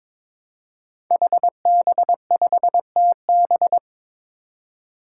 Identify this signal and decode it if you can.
Morse code, heard as a keyed tone, about 22 wpm, 700 Hz. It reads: HB5TB
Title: Five Challenging Letter-Number Combinations 22wpm